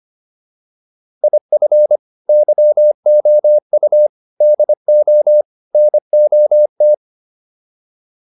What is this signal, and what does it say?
Morse code, heard as a keyed tone, about 25 wpm, 600 Hz. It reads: IF YOU DO NOT